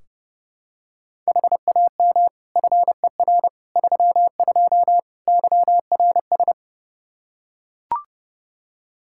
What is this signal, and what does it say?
Morse code, heard as a keyed tone, about 30 wpm, 700 Hz. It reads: HAM FER 32 YRS
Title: Practice QSO - Exchange 3 - More Info - 30wpm